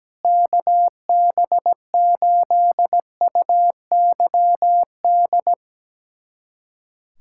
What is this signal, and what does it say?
Morse code, heard as a keyed tone, about 17 wpm, 700 Hz. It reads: KB8UYD
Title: USA Callsign DE Callsign 17wpm